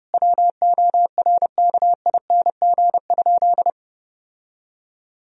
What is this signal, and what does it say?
Morse code, heard as a keyed tone, about 30 wpm, 700 Hz. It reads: WORKING?
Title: Sentences from Top 1000 Words - Review Part 4 - 30wpm